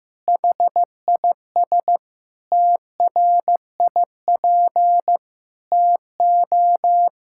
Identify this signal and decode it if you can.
Morse code, heard as a keyed tone, about 15 wpm, 700 Hz. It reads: HIS TRIP TO
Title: Pangram Challenge v2 - 15wpm